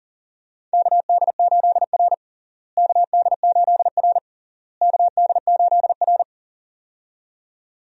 Morse code, heard as a keyed tone, about 37 wpm, 700 Hz. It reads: KD8R KD8R KD8R
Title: USA Call Signs - 4 Characters - Repeated 3x - 40wpm